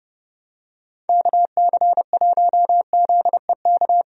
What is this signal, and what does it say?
Morse code, heard as a keyed tone, about 32 wpm, 700 Hz. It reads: KC1ZEK
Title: USA Call Signs - 6 Characters - Repeated 3x - 30wpm